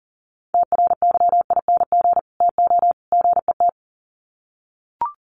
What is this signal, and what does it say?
Morse code, heard as a keyed tone, about 40 wpm, 700 Hz. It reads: TRYING TO GET
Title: Sets of 3 Words 40wpm